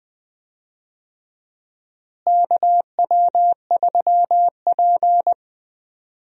Morse code, heard as a keyed tone, about 20 wpm, 700 Hz. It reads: KW3P
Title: USA Call Signs - 4 Characters - Repeated 2x - 20wpm